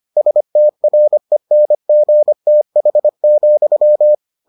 Morse code, heard as a keyed tone, about 25 wpm, 600 Hz. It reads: STRENGTH,